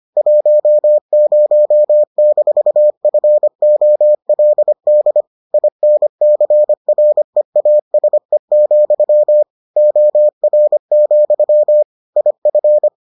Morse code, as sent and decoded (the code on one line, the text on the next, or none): .---- ----- -....- ..-. --- .-.. -.. / .. -. -.-. .-. .- ... --..-- / --- .-. --..-- / .. ..-.
10-FOLD INCREASE, OR, IF